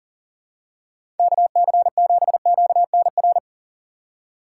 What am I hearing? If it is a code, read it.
Morse code, heard as a keyed tone, about 40 wpm, 700 Hz. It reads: KC7QNR
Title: USA Call Signs - 6 Characters - 3x Character Spacing - 40wpm